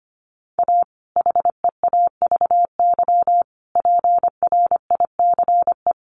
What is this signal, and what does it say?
Morse code, heard as a keyed tone, about 25 wpm, 700 Hz. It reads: A HEAVY PRICE
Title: Sentences from Top 1100 Words 25wpm